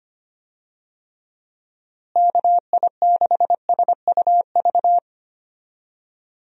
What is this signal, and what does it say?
Morse code, heard as a keyed tone, about 25 wpm, 700 Hz. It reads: KI6SUV
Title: USA Call Signs - 6 Characters - Repeated 2x - 25wpm